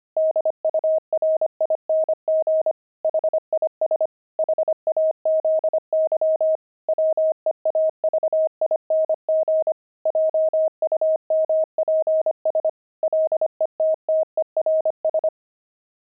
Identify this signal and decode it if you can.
Morse code, heard as a keyed tone, about 25 wpm, 630 Hz. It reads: DURING HIS HAZY WEAVING JUMPS LETTERS
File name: pangram-fs-25-cs-25-weaving